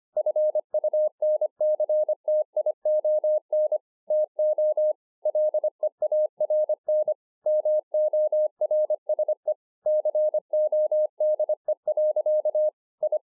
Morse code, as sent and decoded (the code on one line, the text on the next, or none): ..-. ..- -. -.-. - .. --- -. / - --- / .-.. .- .-. -. / -- --- .-. ... / -.-. --- -.. .-.-.- / ..
FUNCTION TO LEARN MORSE CODE. I